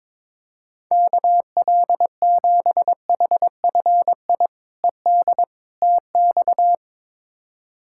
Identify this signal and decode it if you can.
Morse code, heard as a keyed tone, about 22 wpm, 700 Hz. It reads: KL7HFI ED TX
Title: K1USN SST QSO Practice - 22wpm